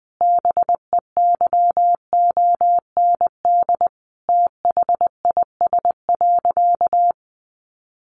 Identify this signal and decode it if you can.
Morse code, heard as a keyed tone, about 20 wpm, 700 Hz. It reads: BEYOND THIS.
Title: Sentences from Top 1200 Words 20wpm